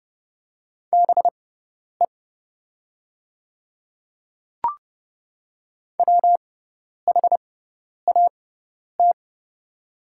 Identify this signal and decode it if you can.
Morse code, heard as a keyed tone, about 31 wpm, spaced as for 10 wpm, 690 Hz. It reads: BE WHAT
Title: 6x Character Spacing - Top 200 Words - 30wpm